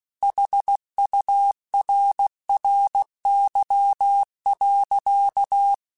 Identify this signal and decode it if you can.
Morse code, heard as a keyed tone, about 16 wpm, 800 Hz. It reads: HURRY.
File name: Technical Subjects / Technical Subjects-mp3s / AND Gate @15